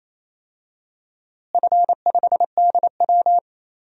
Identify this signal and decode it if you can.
Morse code, heard as a keyed tone, about 27 wpm, 700 Hz. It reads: F5DW
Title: DXCC-100 Call Signs - 4 Characters - Repeated 3x - 28wpm